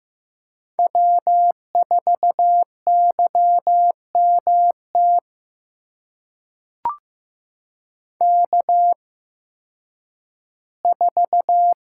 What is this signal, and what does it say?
Morse code, heard as a keyed tone, about 15 wpm, spaced as for 10 wpm, 700 Hz. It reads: W4YMT K4
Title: USA Call Signs - 5 Characters - 8x Character Spacing - 15wpm